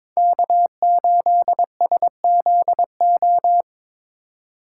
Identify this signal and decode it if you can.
Morse code, heard as a keyed tone, about 22 wpm, 700 Hz. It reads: K8SZO
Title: USA Call Signs - 5 Characters - 4x Character Spacing - 22wpm